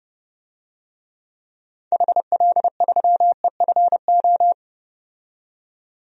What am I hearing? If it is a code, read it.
Morse code, heard as a keyed tone, about 32 wpm, 700 Hz. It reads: HL3EFO
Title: DXCC-100 Call Signs - 6 Characters - 30wpm